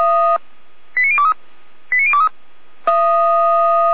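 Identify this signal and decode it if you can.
Five-tone selective call (ZVEI), heard as quick runs of five tones, tones of about 70 ms. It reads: 89023 89023
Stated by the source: Five-tone ZVEI selective call received on an SDR.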